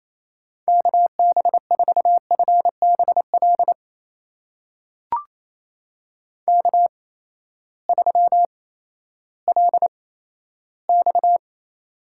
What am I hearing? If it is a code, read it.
Morse code, heard as a keyed tone, about 28 wpm, spaced as for 14 wpm, 700 Hz. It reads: KB4FBL K3LX
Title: USA Call Signs - 8x Character Spacing - 28wpm